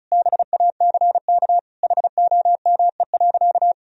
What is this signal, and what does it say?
Morse code, heard as a keyed tone, about 36 wpm, 700 Hz. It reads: BACK HOME.
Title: Sentences from Top 1000 Words - Review Part 4 - 35wpm